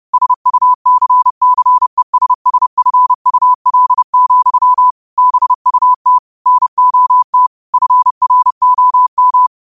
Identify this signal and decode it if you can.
Morse code, heard as a keyed tone, about 30 wpm, 1000 Hz. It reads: SUCCESSFUL, BUT NOT FROM
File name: qotdfiles / 30wpm